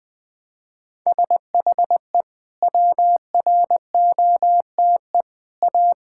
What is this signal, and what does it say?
Morse code, heard as a keyed tone, about 21 wpm, 700 Hz. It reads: SHE WROTE A
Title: Sentences with 2-Word Phrases - 20wpm